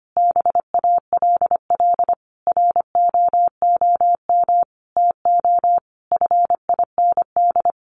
Morse code, heard as a keyed tone, about 25 wpm, 700 Hz. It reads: BALL ROOM TO FIND
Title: Sentences from Top 1300 Words 25wpm